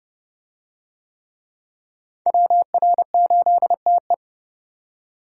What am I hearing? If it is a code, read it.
Morse code, heard as a keyed tone, about 28 wpm, 700 Hz. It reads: WR8TE